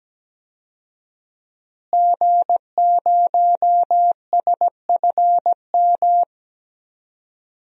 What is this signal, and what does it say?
Morse code, heard as a keyed tone, about 17 wpm, 700 Hz. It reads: G0SFM
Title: DXCC-100 Call Signs - 5 Characters - Repeated 2x - 17wpm